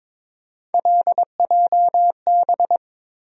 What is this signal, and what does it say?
Morse code, heard as a keyed tone, about 21 wpm, 700 Hz. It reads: LJB
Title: USA Call Sign Postfixes - 3 Letters 22wpm